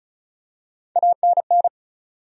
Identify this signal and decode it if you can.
Morse code, heard as a keyed tone, about 36 wpm, 700 Hz. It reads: ANN
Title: USA Common Names - Max 4 Letters - 35wpm